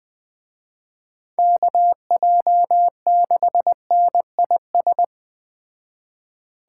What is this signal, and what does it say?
Morse code, heard as a keyed tone, about 21 wpm, 700 Hz. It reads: KJ6NIS